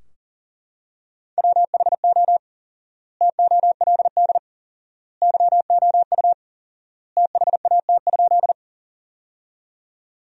Morse code, heard as a keyed tone, about 40 wpm, 700 Hz. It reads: WHO TOLD YOU THAT?
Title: 4x Word-Spacing - Sentences from Top 400 Words - 40wpm